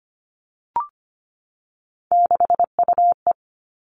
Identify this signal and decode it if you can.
Morse code, heard as a keyed tone, about 25 wpm, 700 Hz. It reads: 6UE
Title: All Cut Numbers - 3-Digit 25wpm